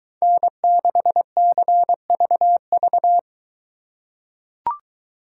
Morse code, heard as a keyed tone, about 23 wpm, 700 Hz. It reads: N6CVV